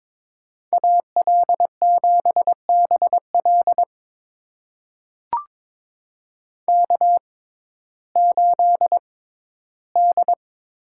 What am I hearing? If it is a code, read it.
Morse code, heard as a keyed tone, about 22 wpm, spaced as for 13 wpm, 700 Hz. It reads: AL7BL K8D
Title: USA Call Signs - 5 Characters - 6x Character Spacing - 22wpm